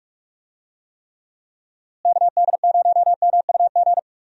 Morse code, heard as a keyed tone, about 41 wpm, 700 Hz. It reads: KD0MUG
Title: USA Call Signs - 6 Characters - Repeated 3x - 45wpm